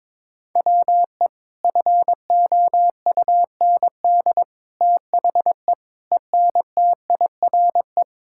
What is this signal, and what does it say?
Morse code, heard as a keyed tone, about 22 wpm, 700 Hz. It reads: WE FOUND THE ENTIRE
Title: Sentences from Top 1000 Words - Review Part 5 - 22wpm